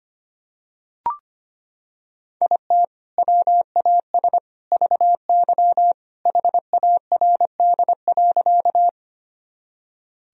Morse code, heard as a keyed tone, about 25 wpm, 700 Hz. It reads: IT WAS VY HARD.
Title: Sentences from Top 300 Words and Abbreviations 25wpm